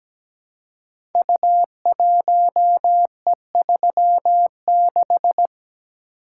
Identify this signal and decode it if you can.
Morse code, heard as a keyed tone, about 17 wpm, 700 Hz. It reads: U1E36
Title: Code Groups - 5 Letter:Numbers - 17wpm